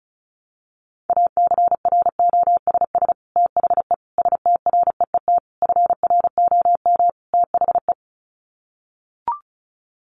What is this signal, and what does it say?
Morse code, heard as a keyed tone, about 34 wpm, 700 Hz. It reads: ACROSS THE STREET FROM THE
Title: Sets of 5 Words Encore 35wpm